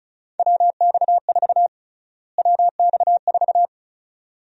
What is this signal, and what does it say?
Morse code, heard as a keyed tone, about 34 wpm, 700 Hz. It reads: WX4 WX4
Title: USA Call Sign Prefixes - Repeated 2x - 35wpm